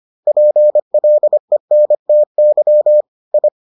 Morse code, heard as a keyed tone, about 25 wpm, 600 Hz. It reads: PLENTY I